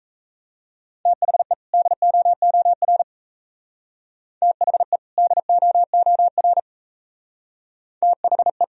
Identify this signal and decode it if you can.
Morse code, heard as a keyed tone, about 33 wpm, 700 Hz. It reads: THE DOOR THE DOOR THE
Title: Speed-Racing - Sets of 2 Words 28wpm